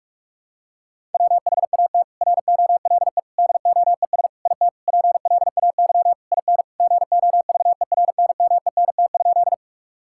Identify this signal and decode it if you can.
Morse code, heard as a keyed tone, about 46 wpm, 700 Hz. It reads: WHAT ROLE DOES IT PLAY IN GOVERNMENT?
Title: Sentences from Top 1000 Words - Review Part 6 - 45wpm